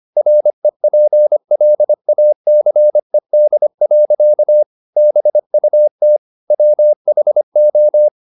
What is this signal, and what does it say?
Morse code, heard as a keyed tone, about 25 wpm, 600 Hz. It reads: REPLACED. BUT WHO
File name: the old man and the sea part 1 25wpm